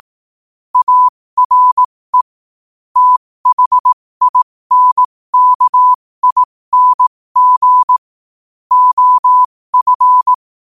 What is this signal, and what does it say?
Morse code, heard as a keyed tone, about 18 wpm, spaced as for 15 wpm, 1000 Hz. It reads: ARE THINKING OF
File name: qotdfiles / 15wpm